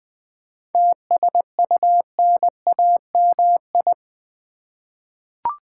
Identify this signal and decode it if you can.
Morse code, heard as a keyed tone, about 20 wpm, 700 Hz. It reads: TSUNAMI
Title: 7 Letter Words - First 3 Letters Given - 20wpm